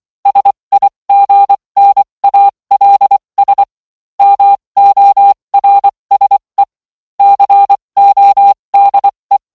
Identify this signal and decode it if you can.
Morse code, heard as a keyed tone, about 24 wpm, spaced as for 20 wpm, 800 Hz. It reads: SIGNALS MORSE CODE